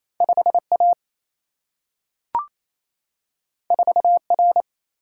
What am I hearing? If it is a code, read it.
Morse code, heard as a keyed tone, about 28 wpm, 700 Hz. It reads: 5A 4R
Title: Single-Letter Single-Number 28wpm